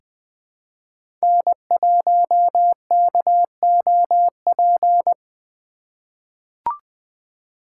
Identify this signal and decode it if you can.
Morse code, heard as a keyed tone, about 20 wpm, 700 Hz. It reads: N1KOP
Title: USA Call Signs - 5 Characters - 3x Character Spacing - 20wpm